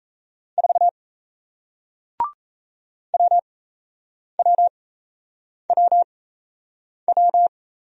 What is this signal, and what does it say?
Morse code, heard as a keyed tone, about 38 wpm, spaced as for 8 wpm, 700 Hz. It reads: 4 WWWW